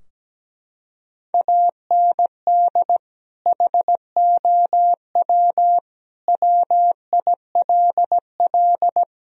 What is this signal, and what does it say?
Morse code, heard as a keyed tone, about 17 wpm, 700 Hz. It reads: AND HOW WILL
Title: Sentences from Top 1000 Words - Review Part 4 - 17wpm